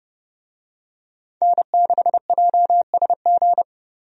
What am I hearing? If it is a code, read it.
Morse code, heard as a keyed tone, about 30 wpm, 700 Hz. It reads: N6JSG
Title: USA Call Signs - 5 Characters - 30wpm